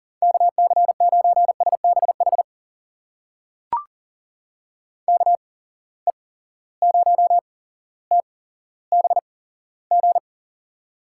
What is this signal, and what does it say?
Morse code, heard as a keyed tone, about 40 wpm, spaced as for 17 wpm, 700 Hz. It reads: KC9SBH KE0TBG
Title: USA Call Signs - 6 Characters - 8x Character Spacing - 40wpm